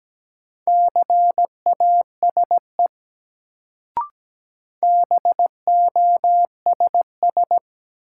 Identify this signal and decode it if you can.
Morse code, heard as a keyed tone, about 17 wpm, 700 Hz. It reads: CASE BOSS
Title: Sentences from Top 700 Words - New Word Drill - 17wpm